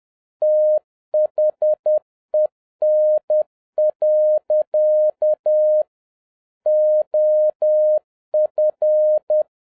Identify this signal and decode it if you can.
Morse code, heard as a keyed tone, about 10 wpm, 610 Hz. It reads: THEN. OF